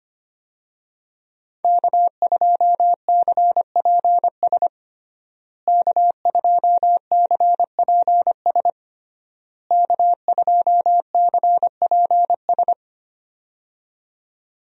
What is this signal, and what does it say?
Morse code, heard as a keyed tone, about 25 wpm, 700 Hz. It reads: K2CPS K2CPS K2CPS
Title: USA Call Signs - 5 Characters - Repeated 3x - 25wpm